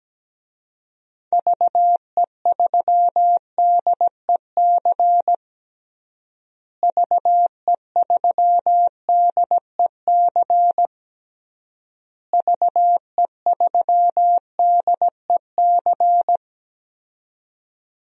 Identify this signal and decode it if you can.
Morse code, heard as a keyed tone, about 17 wpm, 700 Hz. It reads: VE3DEC VE3DEC VE3DEC
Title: DXCC-100 Call Signs - 6 Characters - Repeated 3x - 17wpm